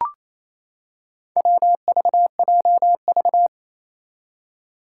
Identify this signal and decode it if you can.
Morse code, heard as a keyed tone, about 28 wpm, 700 Hz. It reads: WVJV
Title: Four Challenging Letter-Number Combinations 28wpm